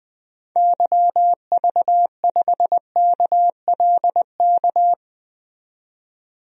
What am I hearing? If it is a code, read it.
Morse code, heard as a keyed tone, about 21 wpm, 700 Hz. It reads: YV5KLK
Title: DXCC-100 Call Signs - 6 Characters - 20wpm